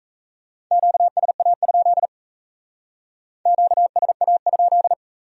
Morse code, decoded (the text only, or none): QSA? QSA?